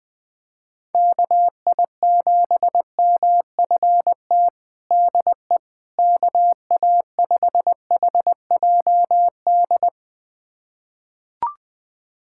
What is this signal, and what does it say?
Morse code, heard as a keyed tone, about 21 wpm, 700 Hz. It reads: KI7MFT DE KA5HJD
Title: USA Callsign DE Callsign - 6 Characters - 20wpm